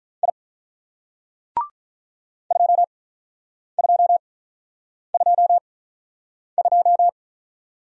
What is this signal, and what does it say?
Morse code, heard as a keyed tone, about 48 wpm, spaced as for 9 wpm, 700 Hz. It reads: I 2222